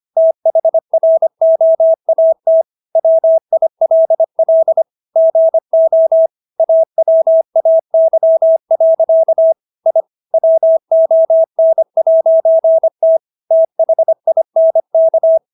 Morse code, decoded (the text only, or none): THROAT WILL GO AWAY. I WON'T THINK